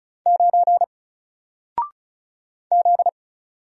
Morse code, heard as a keyed tone, about 33 wpm, 700 Hz. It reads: 9 Z
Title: Single Letter-Number 35wpm